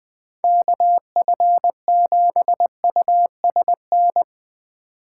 Morse code, heard as a keyed tone, about 20 wpm, 700 Hz. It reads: KF7USN